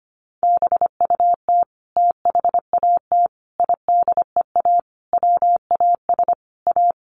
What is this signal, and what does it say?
Morse code, heard as a keyed tone, about 25 wpm, 700 Hz. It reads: BUT THAT IDEA WAS A
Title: Sentences from Top 900 Words 25wpm